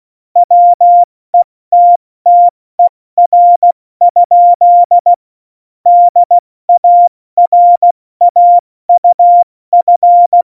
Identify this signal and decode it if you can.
Morse code, heard as a keyed tone, about 16 wpm, 700 Hz. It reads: WETTER? DARAUF
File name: Morsekurs / Witze / Witz 27 Juni 17, Tempo 70 F80